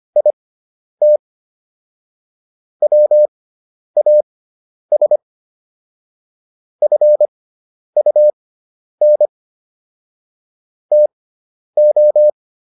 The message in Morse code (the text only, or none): IT WAS FUN TO